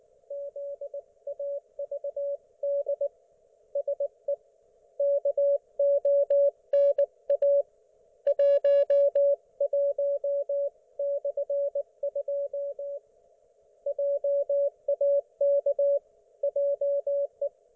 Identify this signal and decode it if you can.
Morse code, heard as a keyed tone, about 18 wpm, 560 Hz. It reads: ZAVD SE KONA 11/2 JAK JE